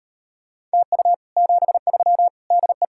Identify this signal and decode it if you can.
Morse code, heard as a keyed tone, about 38 wpm, 700 Hz. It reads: TU 73 DE